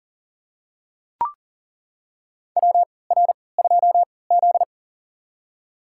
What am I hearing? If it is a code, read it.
Morse code, heard as a keyed tone, about 40 wpm, spaced as for 23 wpm, 700 Hz. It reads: WR2Z
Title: USA Call Signs - 4 Characters - 3x Character Spacing - 40wpm